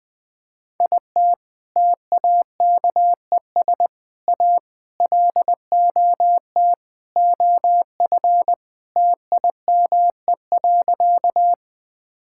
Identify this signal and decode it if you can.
Morse code, heard as a keyed tone, about 20 wpm, 700 Hz. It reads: IT TAKES A LOT OF TIME.
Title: Sentences with 4-Word Phrases - 20wpm